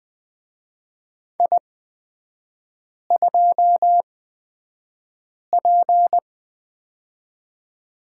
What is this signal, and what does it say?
Morse code, heard as a keyed tone, about 20 wpm, spaced as for 5 wpm, 700 Hz. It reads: I2P
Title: 224 Lesson 36e - BT - Three Characters 20wpm